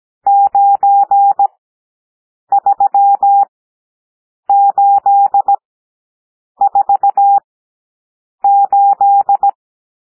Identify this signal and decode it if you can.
Morse code, heard as a keyed tone, about 17 wpm, spaced as for 7 wpm, 800 Hz. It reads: 93848
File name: SOUND NUMBERS 17 wpm